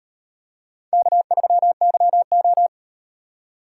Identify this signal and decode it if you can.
Morse code, heard as a keyed tone, about 38 wpm, 700 Hz. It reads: K3YO